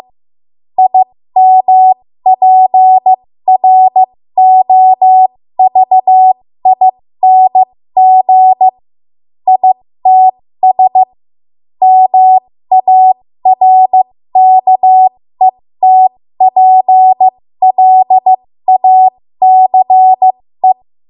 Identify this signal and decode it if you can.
Morse code, heard as a keyed tone, about 15 wpm, spaced as for 13 wpm, 750 Hz. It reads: IMPROVING ITS MARKETPLACE
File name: Morse / 13WPM